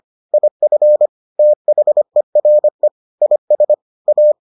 Morse code, heard as a keyed tone, about 25 wpm, 600 Hz. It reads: IF THERE IS A